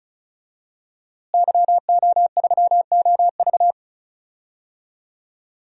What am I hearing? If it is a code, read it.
Morse code, heard as a keyed tone, about 34 wpm, 700 Hz. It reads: YO3OV
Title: DXCC-100 Call Signs - 5 Characters - Repeated 3x - 35wpm